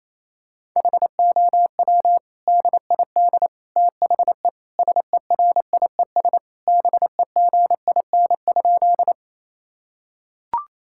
Morse code, heard as a keyed tone, about 28 wpm, 700 Hz. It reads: HOW DID THE SERIES BEGIN?
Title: Sentences from Top 1000 Words - Review Part 5 - 28wpm